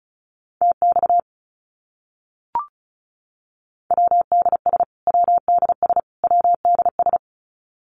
Morse code, heard as a keyed tone, about 34 wpm, 700 Hz. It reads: TX WDS WDS WDS
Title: Abbreviations Repeated 3x - 35wpm